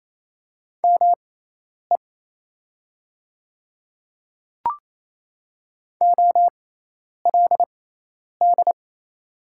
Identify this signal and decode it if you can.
Morse code, heard as a keyed tone, about 28 wpm, spaced as for 10 wpm, 700 Hz. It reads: ME OLD
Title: 6x Character Spacing - Top 200 Words - 28wpm